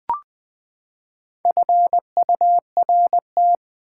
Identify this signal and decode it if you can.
Morse code, heard as a keyed tone, about 20 wpm, 700 Hz. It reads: FURT
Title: Build A Word - 4 Letters At A Time - 11-Letter Words - 20wpm